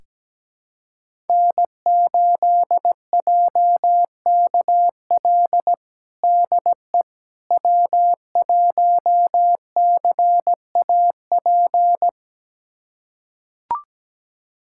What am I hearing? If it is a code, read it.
Morse code, heard as a keyed tone, about 17 wpm, 700 Hz. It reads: N8JKL DE W1CAP
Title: USA Callsign DE Callsign - 5 Characters - 17wpm